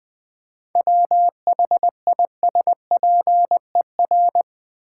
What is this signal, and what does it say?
Morse code, heard as a keyed tone, about 20 wpm, 700 Hz. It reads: WHISPER
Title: 7 Letter Words - First 3 Letters Given - 20wpm